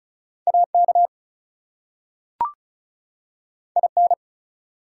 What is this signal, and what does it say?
Morse code, heard as a keyed tone, about 36 wpm, 700 Hz. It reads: AK IN
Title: US States and Canadian Provinces 35wpm